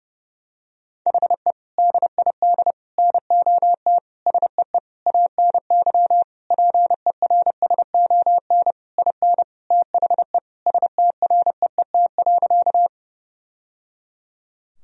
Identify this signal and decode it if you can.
Morse code, heard as a keyed tone, about 28 wpm, 700 Hz. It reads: HE DID NOT SEE ANY PERSON IN THE STREET.
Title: Sentences from Top 1000 Words - Review Part 3 - 30wpm